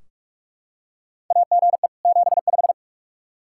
Morse code, heard as a keyed tone, about 46 wpm, 700 Hz. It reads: AGE 75
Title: Practice QSO - Exchange 3 - More Info - 45wpm